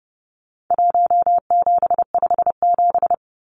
Morse code, heard as a keyed tone, about 30 wpm, 700 Hz. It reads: 1757
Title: Numbers - Contest Style - 4-Digit 30wpm